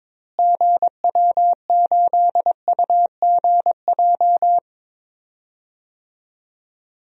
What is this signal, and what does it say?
Morse code, heard as a keyed tone, about 22 wpm, 700 Hz. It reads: GW8UGJ
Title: DXCC-100 Call Signs - 6 Characters - Repeated 3x - 22wpm